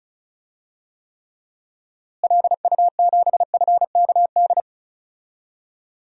Morse code, heard as a keyed tone, about 36 wpm, 700 Hz. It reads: LU7FKD
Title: DXCC-100 Call Signs - 6 Characters - Repeated 2x - 35wpm